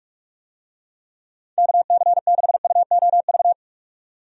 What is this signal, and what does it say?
Morse code, heard as a keyed tone, about 43 wpm, 700 Hz. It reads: KC6UOV
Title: USA Call Signs - 6 Characters - 45wpm